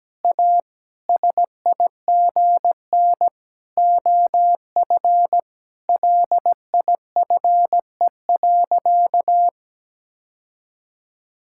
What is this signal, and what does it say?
Morse code, heard as a keyed tone, about 17 wpm, 700 Hz. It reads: A SIGN OF LIFE.
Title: Sentences from Top 1000 Words - Review Part 6 - 17wpm